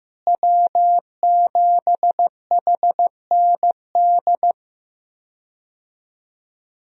Morse code, heard as a keyed tone, about 15 wpm, 700 Hz. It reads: W7HND
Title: USA Call Signs - 5 Characters - Repeated 2x - 15wpm